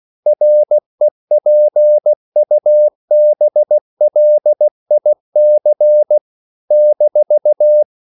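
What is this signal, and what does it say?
Morse code, heard as a keyed tone, about 16 wpm, 600 Hz. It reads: REPUBLIC -
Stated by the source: CW practice of national capitals at 16 words per minute, from ON6ZQ.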